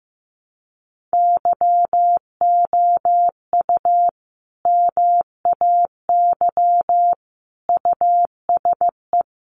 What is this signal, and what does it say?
Morse code, heard as a keyed tone, about 15 wpm, 700 Hz. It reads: YOU MAY USE
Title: Sentences from Top 1400 Words 15wpm